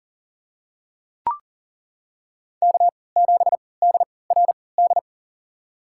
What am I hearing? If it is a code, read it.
Morse code, heard as a keyed tone, about 40 wpm, spaced as for 23 wpm, 700 Hz. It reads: K7DRD
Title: USA Call Signs - 3x Character Spacing - 40wpm